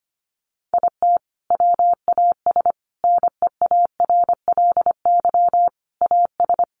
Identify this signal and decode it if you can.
Morse code, heard as a keyed tone, about 25 wpm, 700 Hz. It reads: IT WAS NEARLY AS
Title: Sentences from Top 1400 Words 25wpm